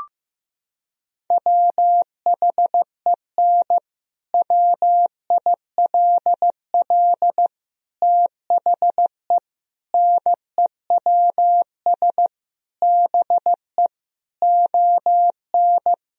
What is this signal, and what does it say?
Morse code, heard as a keyed tone, about 15 wpm, 700 Hz. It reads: WHEN WILL THE NEWS BE ON
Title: Sentences from Top 1000 Words - Review Part 3 - 15wpm